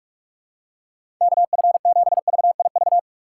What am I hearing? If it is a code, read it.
Morse code, heard as a keyed tone, about 46 wpm, 700 Hz. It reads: KF7VIV
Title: USA Call Signs - 3x Character Spacing - 45wpm